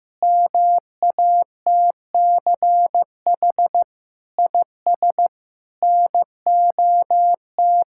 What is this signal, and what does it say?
Morse code, heard as a keyed tone, about 15 wpm, 700 Hz. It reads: MATCH IS NOT
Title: Sentences from Top 1600 Words 15wpm